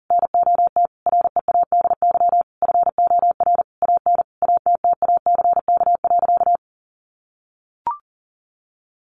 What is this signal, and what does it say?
Morse code, heard as a keyed tone, about 40 wpm, 700 Hz. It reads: NOT READY FOR AN ATTACK.
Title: Sentences from Top 1500 Words 40wpm